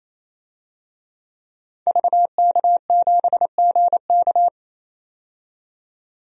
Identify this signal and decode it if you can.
Morse code, heard as a keyed tone, about 28 wpm, 700 Hz. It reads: VK7GK
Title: DXCC-100 Call Signs - 5 Characters - 28wpm